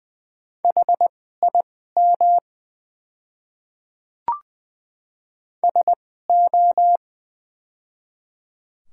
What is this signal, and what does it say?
Morse code, heard as a keyed tone, about 20 wpm, spaced as for 15 wpm, 700 Hz. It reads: HIM SO
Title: 2x Character Spacing - Top 100 Words - 20wpm